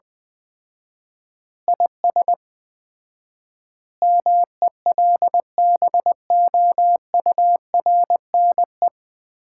Morse code, heard as a keyed tone, about 20 wpm, 700 Hz. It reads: IS MELBOURNE
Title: Common QSO Phrases - 4x Word-Spacing - 20wpm